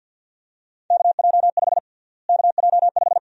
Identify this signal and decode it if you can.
Morse code, heard as a keyed tone, about 51 wpm, 700 Hz. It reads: KJ5 KJ5
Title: USA Call Sign Prefixes - Repeated 2x - 50wpm